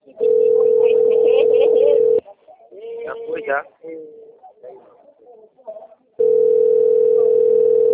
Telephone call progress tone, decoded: ringback tone

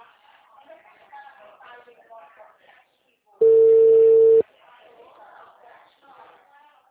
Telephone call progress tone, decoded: ringback tone

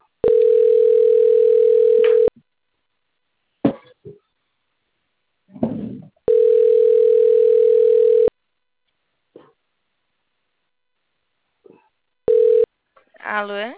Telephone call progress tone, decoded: ringback tone

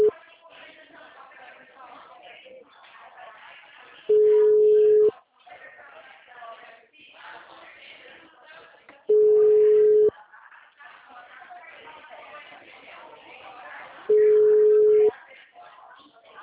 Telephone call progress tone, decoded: ringback tone

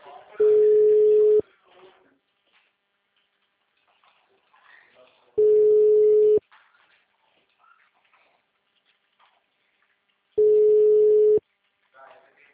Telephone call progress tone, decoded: ringback tone

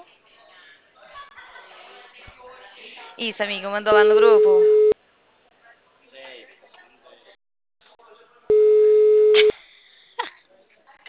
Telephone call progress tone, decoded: ringback tone

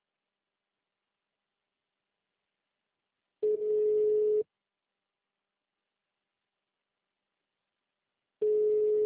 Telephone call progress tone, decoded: ringback tone

